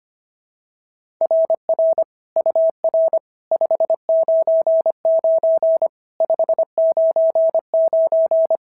Morse code, heard as a keyed tone, about 24 wpm, 650 Hz. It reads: RR UR 599 599
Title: SOTA QSO Practice 25wpm